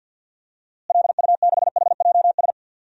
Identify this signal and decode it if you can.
Morse code, heard as a keyed tone, about 52 wpm, 700 Hz. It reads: LU6HJS
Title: DXCC-100 Call Signs - 6 Characters - Repeated 3x - 50wpm